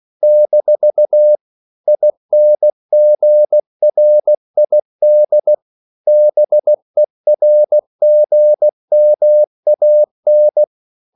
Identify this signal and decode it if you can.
Morse code, heard as a keyed tone, about 16 wpm, 600 Hz. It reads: - INGRID BERGMAN